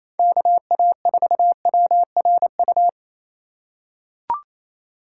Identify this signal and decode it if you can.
Morse code, heard as a keyed tone, about 28 wpm, 700 Hz. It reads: KA4WRU
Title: USA Call Signs - 6 Characters - 3x Character Spacing - 28wpm